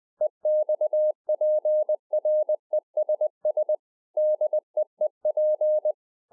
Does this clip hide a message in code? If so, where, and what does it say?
Yes: Morse code from 0.2 to 5.9 s, heard as a keyed tone, about 20 wpm, 610 Hz. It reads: EXPRESS DEEP